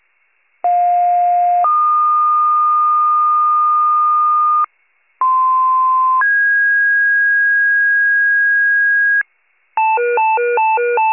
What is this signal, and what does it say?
Two-tone page, heard as a one-second tone, then a three-second tone, about 710 then 1220 Hz, about 1040 then 1650 Hz, with noise. It is heard from 0.6 to 9.2 s.
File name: pager / 46 RESCUE / wcalert